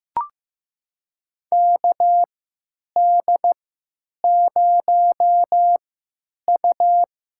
Morse code, heard as a keyed tone, about 15 wpm, spaced as for 9 wpm, 700 Hz. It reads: KD0U
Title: USA Call Signs - 3x Character Spacing - 15wpm